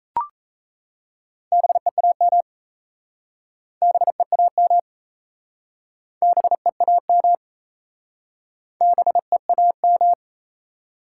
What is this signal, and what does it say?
Morse code, heard as a keyed tone, about 37 wpm, 700 Hz. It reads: BEAM BEAM BEAM BEAM